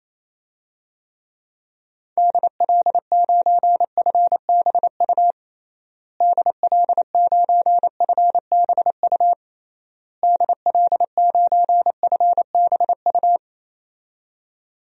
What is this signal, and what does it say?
Morse code, heard as a keyed tone, about 28 wpm, 700 Hz. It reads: DL9FBU DL9FBU DL9FBU
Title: DXCC-100 Call Signs - 6 Characters - Repeated 3x - 28wpm